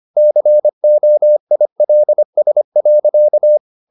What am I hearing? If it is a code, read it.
Morse code, heard as a keyed tone, about 25 wpm, 600 Hz. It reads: COILS.
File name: the old man and the sea part 1 25wpm